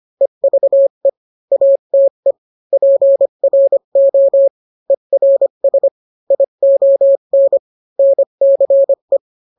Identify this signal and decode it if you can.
Morse code, heard as a keyed tone, about 25 wpm, 550 Hz. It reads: EVE ATE PRO ERS ION NCE